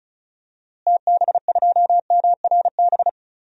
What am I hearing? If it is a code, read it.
Morse code, heard as a keyed tone, about 34 wpm, 700 Hz. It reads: TB2MRB
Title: DXCC-100 Call Signs - 6 Characters - Repeated 2x - 35wpm